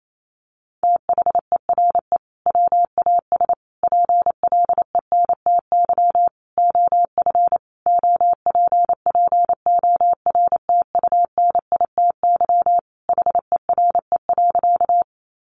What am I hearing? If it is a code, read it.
Morse code, heard as a keyed tone, about 28 wpm, 700 Hz. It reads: THERE WAS PLENTY OF OPPORTUNITY HERE.
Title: Sentences from Top 1300 Words 28wpm